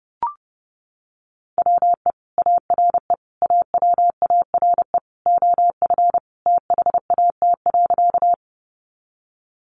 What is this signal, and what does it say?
Morse code, heard as a keyed tone, about 30 wpm, 700 Hz. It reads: WE ARE AWARE OF THAT.
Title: Sentences from Top 1200 Words 30wpm